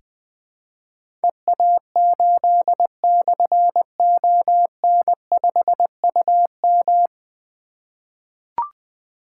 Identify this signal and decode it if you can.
Morse code, heard as a keyed tone, about 20 wpm, 700 Hz. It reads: EA8/ON5UM
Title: Difficult Call Signs 20wpm